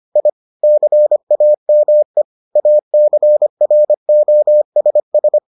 Morse code, heard as a keyed tone, about 25 wpm, 600 Hz. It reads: I CAME ACROSS